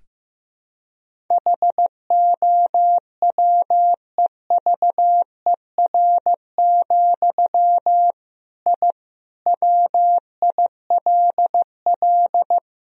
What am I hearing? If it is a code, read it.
Morse code, heard as a keyed tone, about 15 wpm, 700 Hz. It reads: HOWEVER, I WILL
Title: Sentences from Top 1000 Words - Review Part 1 - 15wpm